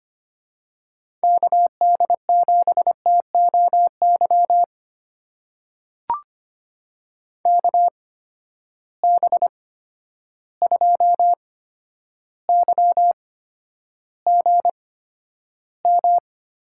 Morse code, heard as a keyed tone, about 25 wpm, spaced as for 11 wpm, 700 Hz. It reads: KD7TOY KB2YGM
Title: USA Call Signs - 6 Characters - 8x Character Spacing - 25wpm